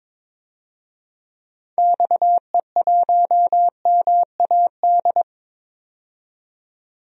Morse code, heard as a keyed tone, about 22 wpm, 700 Hz. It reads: XE1MAD
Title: DXCC-100 Call Signs - 6 Characters - 22wpm